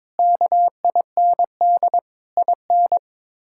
Morse code, heard as a keyed tone, about 22 wpm, 700 Hz. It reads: KIND IN
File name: practice / 22wpm / Sets of 6 Words - 22wpm